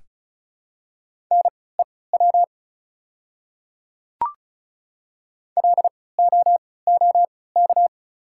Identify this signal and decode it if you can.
Morse code, heard as a keyed tone, about 34 wpm, spaced as for 20 wpm, 700 Hz. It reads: NEW LOOK
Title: 3x Character Spacing - Top 300 Words - 35wpm